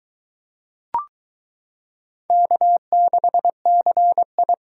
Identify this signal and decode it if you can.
Morse code, heard as a keyed tone, about 22 wpm, 700 Hz. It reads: K6CI